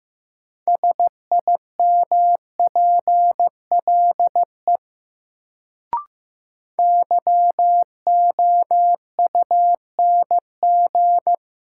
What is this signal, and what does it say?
Morse code, heard as a keyed tone, about 15 wpm, 700 Hz. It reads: SIMPLE YOUNG
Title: Sentences from Top 400 Words - New Word Drill - 15wpm